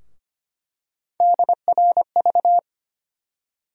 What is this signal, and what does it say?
Morse code, heard as a keyed tone, about 25 wpm, 700 Hz. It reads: DRV
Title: USA Call Sign Postfixes - 3 Letters 25wpm